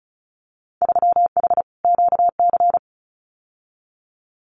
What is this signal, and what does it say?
Morse code, heard as a keyed tone, about 34 wpm, 690 Hz. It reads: 3H QC
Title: Winter Field Day Exchanges 35wpm